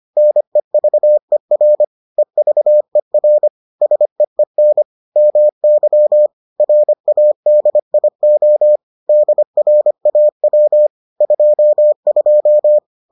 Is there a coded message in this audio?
Yes: Morse code from 0.2 to 12.8 s, heard as a keyed tone, about 25 wpm, 600 Hz. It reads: NEVER EVER SEEN MY RADIO DRAW 22